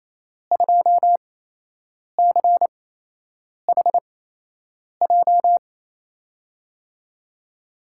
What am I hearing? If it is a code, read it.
Morse code, heard as a keyed tone, about 28 wpm, spaced as for 8 wpm, 700 Hz. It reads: 2CHJ
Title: USA Call Signs - 6 Characters - 8x Character Spacing - 28wpm